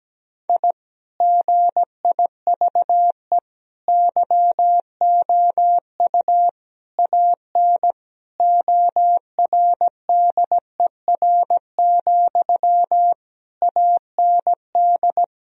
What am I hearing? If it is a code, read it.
Morse code, heard as a keyed tone, about 17 wpm, 700 Hz. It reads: I GIVE YOU AN ORDER, AND
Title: Sentences from Top 1000 Words - Review Part 2 - 17wpm